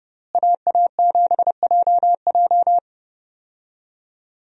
Morse code, heard as a keyed tone, about 31 wpm, 700 Hz. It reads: AA7JJ
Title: USA Call Signs - 5 Characters - Repeated 3x - 30wpm